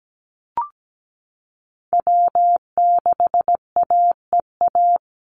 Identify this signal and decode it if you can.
Morse code, heard as a keyed tone, about 17 wpm, 700 Hz. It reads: W6AEA
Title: Call Signs Repeated 2x - 17wpm